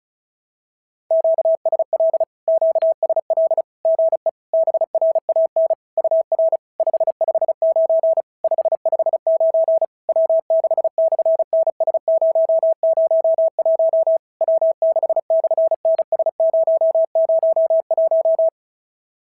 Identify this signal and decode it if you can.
Morse code, heard as a keyed tone, about 36 wpm, 650 Hz. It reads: QSL QSL GE BRAN UR 559 559 W6/NS001 W6/NS001
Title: SOTA QSO Practice 35wpm